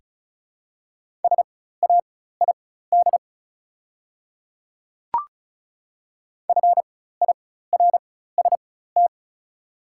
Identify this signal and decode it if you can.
Morse code, heard as a keyed tone, about 34 wpm, spaced as for 16 wpm, 700 Hz. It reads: SAID FIRST
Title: 4x Character Spacing - Top 100 Words - 35wpm